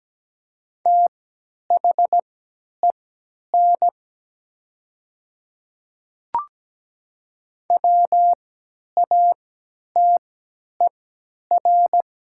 Morse code, heard as a keyed tone, about 17 wpm, spaced as for 10 wpm, 700 Hz. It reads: THEN WATER
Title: 3x Character Spacing - Top 100 Words - 17wpm